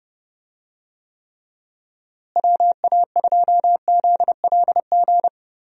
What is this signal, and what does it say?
Morse code, heard as a keyed tone, about 30 wpm, 700 Hz. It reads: WA2ZLG
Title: USA Call Signs - 6 Characters - Repeated 3x - 30wpm